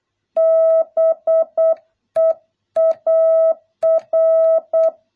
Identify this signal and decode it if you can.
Morse code, heard as a keyed tone, about 8 wpm, 650 Hz. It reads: BEAR